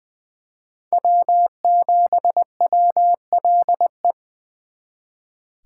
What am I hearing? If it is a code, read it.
Morse code, heard as a keyed tone, about 24 wpm, 700 Hz. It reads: W7WLE
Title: Speed Racing - USA 1x3 Call Signs 20wpm